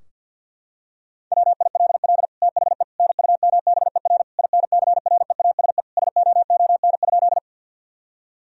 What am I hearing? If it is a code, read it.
Morse code, heard as a keyed tone, about 50 wpm, 700 Hz. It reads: WILL THE NUMBER INCREASE SOON?
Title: Sentences from Top 1000 Words - Review Part 4 - 50wpm